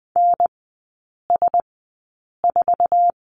Morse code, heard as a keyed tone, about 20 wpm, spaced as for 8 wpm, 700 Hz. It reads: NS4
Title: 100 Lesson 17e - Letter u - Three Characters 20wpm